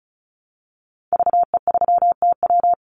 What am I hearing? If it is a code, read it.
Morse code, heard as a keyed tone, about 36 wpm, 700 Hz. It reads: VE3TW
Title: Call Signs - 35wpm